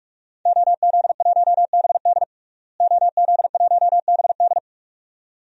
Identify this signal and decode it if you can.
Morse code, heard as a keyed tone, about 48 wpm, 700 Hz. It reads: OZ1BD OZ1BD
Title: DXCC-100 Call Signs - 5 Characters - Repeated 2x - 45wpm